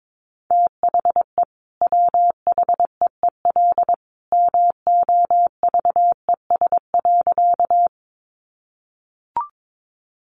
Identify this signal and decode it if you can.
Morse code, heard as a keyed tone, about 22 wpm, 700 Hz. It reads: THE WHEEL MOVES.
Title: Additional Easy Phrases 22wpm